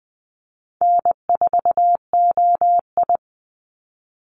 Morse code, heard as a keyed tone, about 20 wpm, 700 Hz. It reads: N4OI
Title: Easy Call Signs Repeated 3x - 20wpm